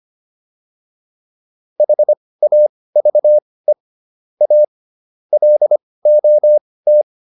Morse code, heard as a keyed tone, about 25 wpm, spaced as for 18 wpm, 600 Hz. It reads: HAVE A LOT